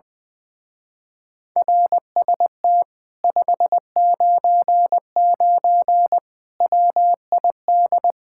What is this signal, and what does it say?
Morse code, heard as a keyed tone, about 21 wpm, 700 Hz. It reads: RST 599 WID
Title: Speed-Racing Common QSO Phrases - 20wpm